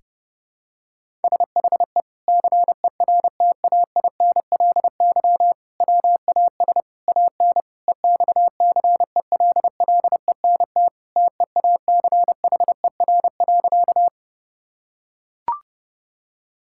Morse code, heard as a keyed tone, about 30 wpm, 700 Hz. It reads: SHE CERTAINLY WAS AN EXCELLENT TEACHER.
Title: Sentences from Top 1000 Words - Review Part 6 - 30wpm